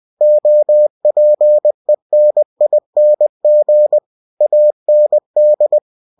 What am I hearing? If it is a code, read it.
Morse code, heard as a keyed tone, about 20 wpm, 600 Hz. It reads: OPENING AND
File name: cw-books / winnie-the-pooh / winnie-20wpm-page-0064